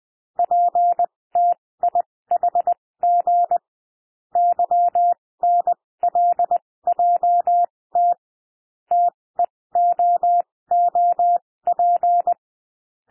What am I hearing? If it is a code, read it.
Morse code, heard as a keyed tone, about 20 wpm, spaced as for 16 wpm, 700 Hz. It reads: PTIHG YNLJT TEOOP
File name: cwmp3 / L1 SOUND LETTERS 20 wpm